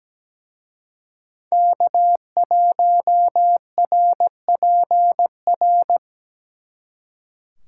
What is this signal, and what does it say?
Morse code, heard as a keyed tone, about 20 wpm, 700 Hz. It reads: K1RPR
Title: Speed Racing - USA Call Signs 17wpm